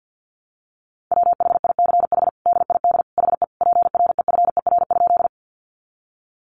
Morse code, heard as a keyed tone, about 49 wpm, 700 Hz. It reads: WHICH DID HE PREFER?
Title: Sentences from Top 1100 Words 50wpm